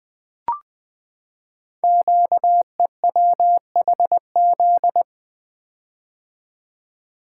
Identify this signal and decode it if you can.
Morse code, heard as a keyed tone, about 20 wpm, 700 Hz. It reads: QEWHZ
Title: Code Groups - 5 Letter:Numbers - 20wpm